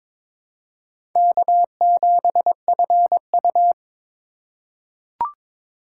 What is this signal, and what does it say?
Morse code, heard as a keyed tone, about 22 wpm, 700 Hz. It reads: K7FU
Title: USA Call Signs - 2x Character Spacing - 22wpm